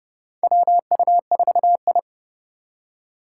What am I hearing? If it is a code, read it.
Morse code, heard as a keyed tone, about 29 wpm, 700 Hz. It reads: WU4I
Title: USA Call Signs - 4 Characters - 6x Character Spacing - 30wpm